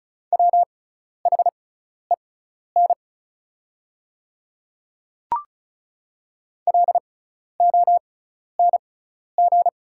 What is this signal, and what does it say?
Morse code, heard as a keyed tone, about 34 wpm, spaced as for 12 wpm, 690 Hz. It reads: WHEN LONG
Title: 6x Character Spacing - Top 100 Words - 35wpm